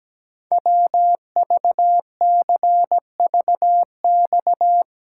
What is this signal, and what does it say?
Morse code, heard as a keyed tone, about 17 wpm, 700 Hz. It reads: WVCVX
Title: Code Groups - 5 Letter:Numbers - 17wpm